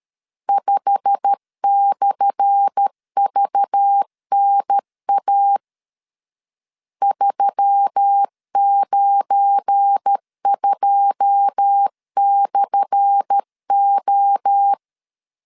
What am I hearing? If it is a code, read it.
Morse code, heard as a keyed tone, about 13 wpm, 800 Hz. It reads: 5/VNA 392/O
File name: LEZIONI / 10 SIGNS / 10A SLASH 12wpm 2314